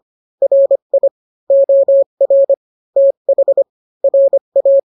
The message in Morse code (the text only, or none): RI OR TH RA